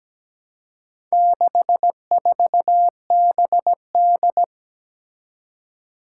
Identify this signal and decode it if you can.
Morse code, heard as a keyed tone, about 17 wpm, 700 Hz. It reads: 64BD
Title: Four Challenging Letter-Number Combinations 17wpm